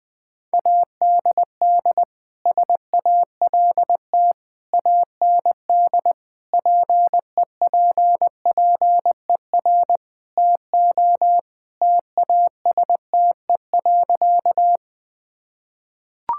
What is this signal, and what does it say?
Morse code, heard as a keyed tone, about 21 wpm, 700 Hz. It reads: ADD SALT AND PEPPER TO TASTE.
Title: Sentences with 5-Word Phrases - 20wpm